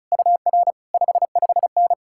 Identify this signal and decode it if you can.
Morse code, heard as a keyed tone, about 48 wpm, spaced as for 41 wpm, 700 Hz. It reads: UR 55N
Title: Speed-Racing Common QSO Phrases - 35wpm